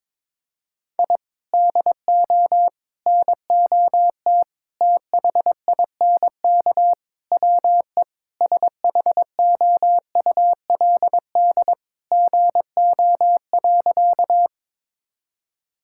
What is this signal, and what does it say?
Morse code, heard as a keyed tone, about 22 wpm, 700 Hz. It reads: I DO NOT THINK WE SHOULD GO.
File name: practice / 22wpm / Sentences with 4-Word Phrases - 22wpm